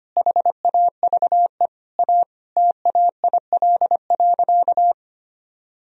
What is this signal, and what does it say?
Morse code, heard as a keyed tone, about 25 wpm, 700 Hz. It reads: HAVE A TAIL.
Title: Sentences from Top 1000 Words - Review Part 3 - 25wpm